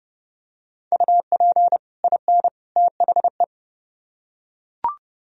Morse code, heard as a keyed tone, about 29 wpm, 700 Hz. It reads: UP IN THE